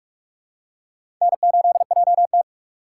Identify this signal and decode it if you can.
Morse code, heard as a keyed tone, about 44 wpm, 700 Hz. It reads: N8JT
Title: USA Call Signs - 4 Characters - 4x Character Spacing - 45wpm